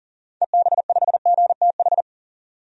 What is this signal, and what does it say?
Morse code, heard as a keyed tone, about 40 wpm, 700 Hz. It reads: EB5GTH
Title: DXCC-100 Call Signs - 6 Characters - 40wpm